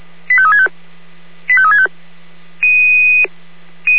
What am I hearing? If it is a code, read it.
Five-tone selective call (ZVEI), heard as quick runs of five tones, tones of about 70 ms. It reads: 85365 85365
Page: ZVEI